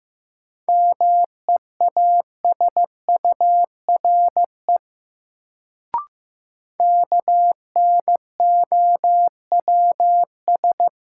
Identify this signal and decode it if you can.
Morse code, heard as a keyed tone, about 15 wpm, 700 Hz. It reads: MEASURE KNOWS
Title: Sentences from Top 1000 Words - Review Part 1 - New Word Drill - 15wpm